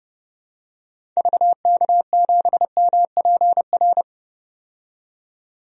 Morse code, heard as a keyed tone, about 32 wpm, 700 Hz. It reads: VK7MPR
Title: DXCC-100 Call Signs - 6 Characters - 30wpm